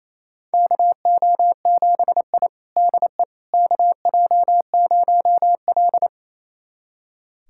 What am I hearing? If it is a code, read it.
Morse code, heard as a keyed tone, about 28 wpm, 700 Hz. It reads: KO7I DE KJ0L
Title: USA Callsign DE Callsign - 4 Characters - 28wpm